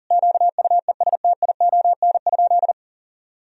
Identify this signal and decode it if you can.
Morse code, heard as a keyed tone, about 40 wpm, 700 Hz. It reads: QUESTION?
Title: Sentences from Top 1000 Words - Review Part 4 - 40wpm